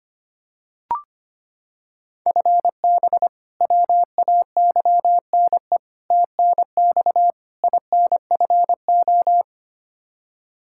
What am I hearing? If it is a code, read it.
Morse code, heard as a keyed tone, about 25 wpm, 700 Hz. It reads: FB WAYNE TNX INFO
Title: Practice QSO - Exchange 3 - More Info - 25wpm